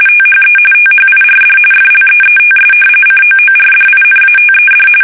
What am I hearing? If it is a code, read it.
RTTY, about 45.45 baud. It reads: THE THIRD FLEETS COORDINATED
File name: fsk850 / battleshipnj